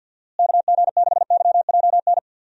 Morse code, heard as a keyed tone, about 46 wpm, 700 Hz. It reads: KG6YJN
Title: USA Call Signs - 6 Characters - Repeated 2x - 50wpm